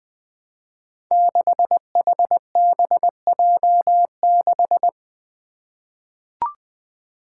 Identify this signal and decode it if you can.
Morse code, heard as a keyed tone, about 20 wpm, 700 Hz. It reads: 6HBJ6
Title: Five Challenging Letter-Number Combinations 20wpm